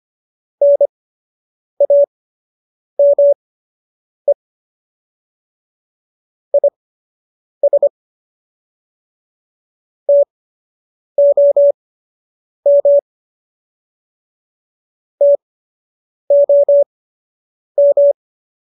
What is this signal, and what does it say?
Morse code, heard as a keyed tone, about 25 wpm, spaced as for 8 wpm, 570 Hz. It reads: NAME IS TOM TOM